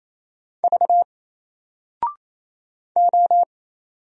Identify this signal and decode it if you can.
Morse code, heard as a keyed tone, about 27 wpm, 700 Hz. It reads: V O
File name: Single-Letter-Number-28wpm